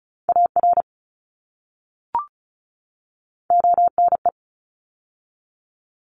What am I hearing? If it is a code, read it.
Morse code, heard as a keyed tone, about 36 wpm, 700 Hz. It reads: AR ONE
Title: ARRL Sections 35wpm